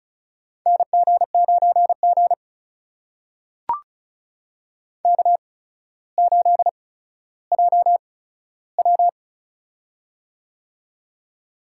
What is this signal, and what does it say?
Morse code, heard as a keyed tone, about 33 wpm, spaced as for 15 wpm, 700 Hz. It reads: NG9G K8JW
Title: USA Call Signs - 4 Characters - 8x Character Spacing - 35wpm